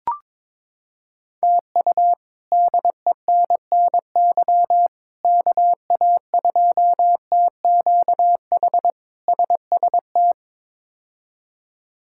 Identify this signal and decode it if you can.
Morse code, heard as a keyed tone, about 22 wpm, 700 Hz. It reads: TU DENNY KA2TQH SST
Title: K1USN SST QSO Practice - 22wpm